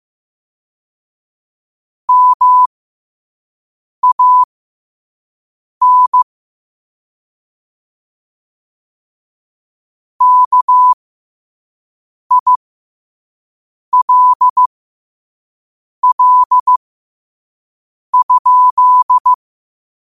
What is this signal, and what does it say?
Morse code, heard as a keyed tone, about 15 wpm, spaced as for 5 wpm, 1000 Hz. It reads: MAN KILL?
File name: qotdfiles / 5wpm